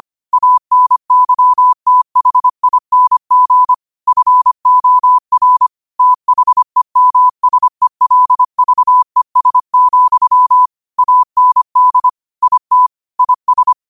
Morse code, heard as a keyed tone, about 25 wpm, 1000 Hz. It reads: ANYTHING FOR THEMSELVES, AND IT IS
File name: qotdfiles / 25wpm